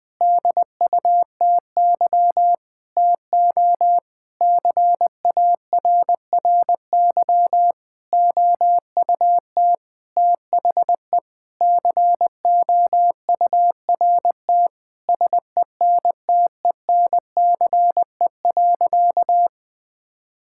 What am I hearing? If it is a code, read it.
Morse code, heard as a keyed tone, about 21 wpm, 700 Hz. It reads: DUTY TO CARRY OUT THE COURT SENTENCE.
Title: Sentences from Top 1600 Words 20wpm